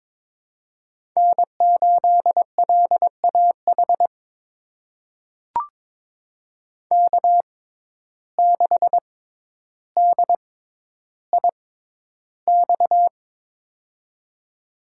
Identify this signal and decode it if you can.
Morse code, heard as a keyed tone, about 22 wpm, spaced as for 11 wpm, 700 Hz. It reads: N8LAH K6DIX
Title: USA Call Signs - 5 Characters - 6x Character Spacing - 22wpm